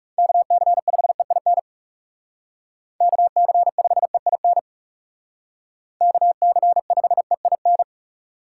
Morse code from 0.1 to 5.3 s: KC5EIN KC5EIN

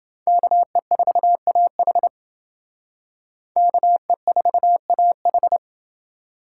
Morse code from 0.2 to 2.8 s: KE4AH

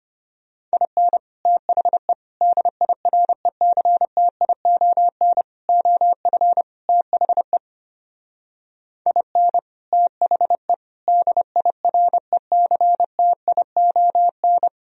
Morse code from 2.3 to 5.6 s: DIRECTION

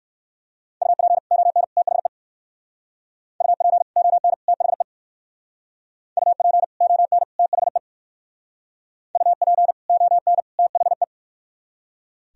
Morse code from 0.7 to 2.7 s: UP ON THE

